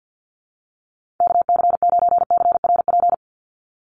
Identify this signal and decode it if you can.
Morse code, heard as a keyed tone, about 49 wpm, 700 Hz. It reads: KC9CRP
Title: USA Call Signs 50wpm